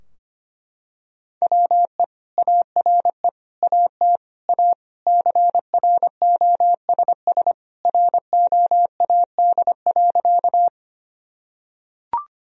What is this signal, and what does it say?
Morse code, heard as a keyed tone, about 25 wpm, 700 Hz. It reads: WE ARE AT A CROSS ROAD.
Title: Sentences from Top 1000 Words - Review Part 2 - 25wpm